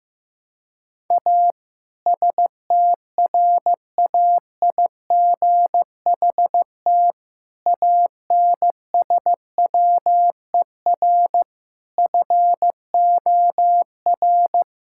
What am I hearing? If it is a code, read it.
Morse code, heard as a keyed tone, about 15 wpm, 700 Hz. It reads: A STRAIGHT ANSWER FOR
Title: Sentences from Top 1000 Words - Review Part 6 - 15wpm